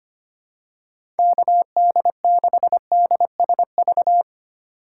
Morse code, heard as a keyed tone, about 25 wpm, 700 Hz. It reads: KD6DSV
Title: USA Call Signs - 2x Character Spacing - 25wpm